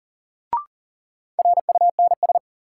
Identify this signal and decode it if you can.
Morse code, heard as a keyed tone, about 38 wpm, 700 Hz. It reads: RUNS
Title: Sentences from Top 600 Words - New Word Drill - 40wpm